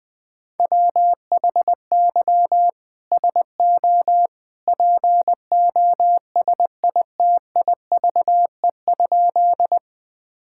Morse code, heard as a keyed tone, about 20 wpm, 700 Hz. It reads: WHY SO POSITIVE?
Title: Sentences from Top 1000 Words - Review Part 5 - 20wpm